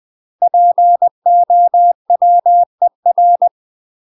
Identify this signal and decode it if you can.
Morse code, heard as a keyed tone, about 20 wpm, 700 Hz. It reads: POWER
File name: UK MASTER QSOs TRAINING / 20WPM / UK MASTER QSOs 0028